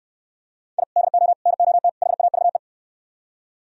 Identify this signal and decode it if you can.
Morse code, heard as a keyed tone, about 69 wpm, 700 Hz. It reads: I DO NOT HAVE